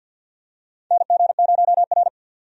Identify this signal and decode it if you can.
Morse code, heard as a keyed tone, about 47 wpm, 700 Hz. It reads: NG0R
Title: USA Call Signs - 4 Characters - 50wpm